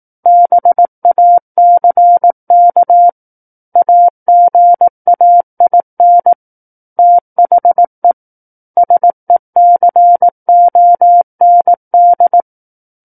Morse code, heard as a keyed tone, about 18 wpm, 700 Hz. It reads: BACK AGAIN THE SECOND
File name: Book One / 09-Chapter 9 - 18 WPM